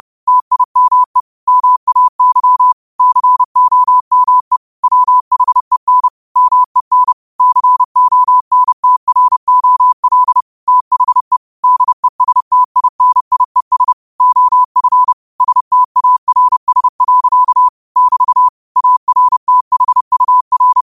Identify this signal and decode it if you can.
Morse code, heard as a keyed tone, about 30 wpm, 1000 Hz. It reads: TIME MAY COME WHEN MEN CONTROL THE DESTINIES OF STARS. <BT> ARTHUR
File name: qotdfiles / 30wpm